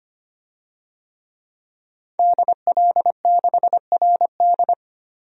Morse code, heard as a keyed tone, about 25 wpm, 700 Hz. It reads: DL6RD
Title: DXCC-100 Call Signs - 5 Characters - Repeated 2x - 25wpm